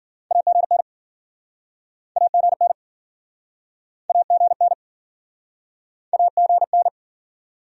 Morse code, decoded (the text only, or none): AGN AGN AGN AGN